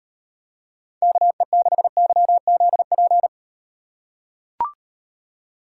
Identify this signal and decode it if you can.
Morse code, heard as a keyed tone, about 38 wpm, 700 Hz. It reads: KE6YZP